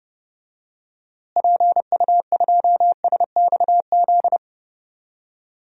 Morse code, heard as a keyed tone, about 32 wpm, 700 Hz. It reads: PU2SXZ
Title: DXCC-100 Call Signs - 6 Characters - Repeated 2x - 30wpm